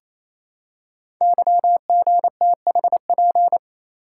Morse code, heard as a keyed tone, about 28 wpm, 700 Hz. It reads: YGTHP
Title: Code Groups - 5 Letters - 28wpm